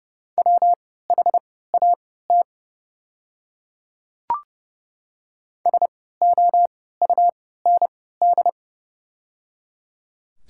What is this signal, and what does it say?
Morse code, heard as a keyed tone, about 30 wpm, spaced as for 17 wpm, 700 Hz. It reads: WHAT SOUND
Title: 3x Character Spacing - Top 100 Words - 30wpm